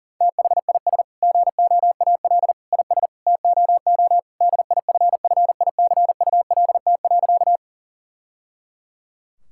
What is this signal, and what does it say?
Morse code, heard as a keyed tone, about 40 wpm, 700 Hz. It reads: THIS GOAL IS TOO DIFFICULT.
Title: Sentences from Top 1000 Words - Review Part 5 - 40wpm